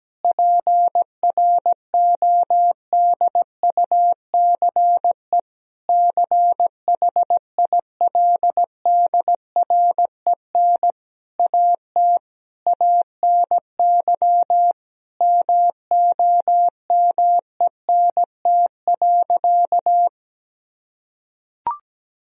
Morse code, heard as a keyed tone, about 17 wpm, 700 Hz. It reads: PRODUCE CHILDREN AT ANY MOMENT.
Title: Sentences from Top 1000 Words - Review Part 3 - 17wpm